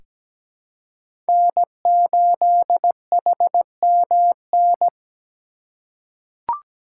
Morse code, heard as a keyed tone, about 17 wpm, 700 Hz. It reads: N8HMN
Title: USA Call Signs - 5 Characters - 4x Character Spacing - 17wpm